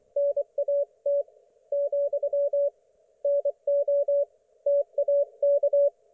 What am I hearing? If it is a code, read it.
Morse code, heard as a keyed tone, about 24 wpm, 570 Hz. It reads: NAT , NO TAK
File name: Archiv / CUC224 / ok1fhi cul224